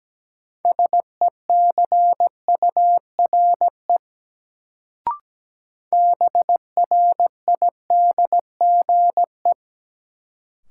Morse code, heard as a keyed tone, about 17 wpm, 700 Hz. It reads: SECURE BRIDGE
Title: Sentences from Top 1200 Words - New Word Drill Alt - 17wpm